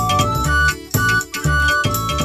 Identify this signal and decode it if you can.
There is no code.